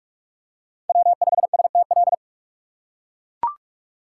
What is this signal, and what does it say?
Morse code, heard as a keyed tone, about 43 wpm, 700 Hz. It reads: W5STL
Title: USA Call Signs - 5 Characters - 2x Character Spacing - 45wpm